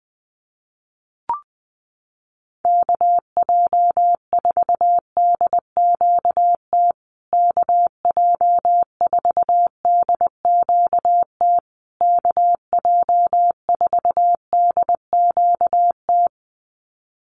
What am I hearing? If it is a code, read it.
Morse code, heard as a keyed tone, about 20 wpm, 700 Hz. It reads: KJ4DQT KJ4DQT KJ4DQT
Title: USA Call Signs Repeated 3x 20wpm